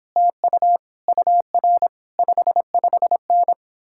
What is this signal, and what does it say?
Morse code, heard as a keyed tone, about 29 wpm, 700 Hz. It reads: TU UR 55N